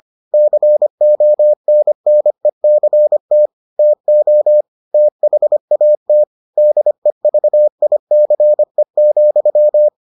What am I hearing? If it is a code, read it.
Morse code, heard as a keyed tone, about 25 wpm, 600 Hz. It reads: CONNECT TO THAT DEVICE,